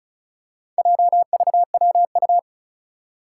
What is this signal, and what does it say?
Morse code, heard as a keyed tone, about 36 wpm, 700 Hz. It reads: JVWU
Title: Four Challenging Letter-Number Combinations 35wpm